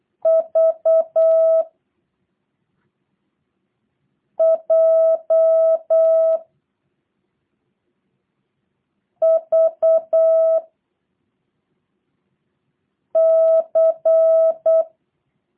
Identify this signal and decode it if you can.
Morse code, heard as a keyed tone, about 8 wpm, 650 Hz. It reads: V J V C
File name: jh8cbh / 2023cw / review1-5